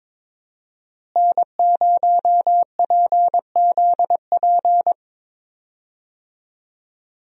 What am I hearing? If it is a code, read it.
Morse code, heard as a keyed tone, about 22 wpm, 700 Hz. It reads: N0PZP
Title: USA Call Signs - 5 Characters - Repeated 3x - 22wpm